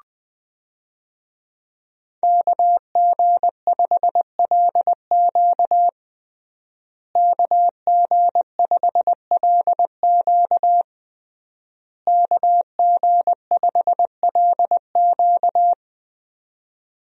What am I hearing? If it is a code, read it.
Morse code, heard as a keyed tone, about 20 wpm, 700 Hz. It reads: KG5LQ KG5LQ KG5LQ
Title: USA Call Signs - 5 Characters - Repeated 3x - 20wpm